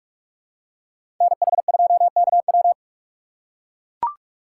Morse code, heard as a keyed tone, about 44 wpm, 690 Hz. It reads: NH2KW
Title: USA Call Signs - 5 Characters - 8x Character Spacing - 45wpm